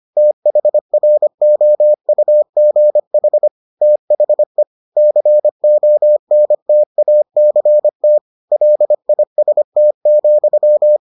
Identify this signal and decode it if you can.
Morse code, heard as a keyed tone, about 25 wpm, 600 Hz. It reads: THROUGH THE CONTACT LIST,